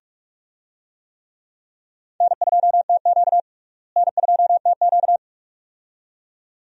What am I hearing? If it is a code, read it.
Morse code, heard as a keyed tone, about 42 wpm, 700 Hz. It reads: N2TQ N2TQ
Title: USA Call Signs - 4 Characters - Repeated 2x - 45wpm